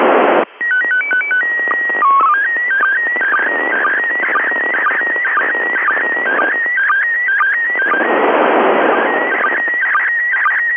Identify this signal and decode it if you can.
SSTV (slow-scan television), PD 120.
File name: dc9dd / isssstv